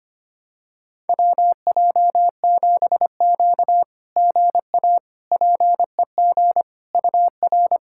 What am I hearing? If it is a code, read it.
Morse code, heard as a keyed tone, about 25 wpm, 700 Hz. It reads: WJ7Q GA PEG UR
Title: SOTA QSO Practice 25wpm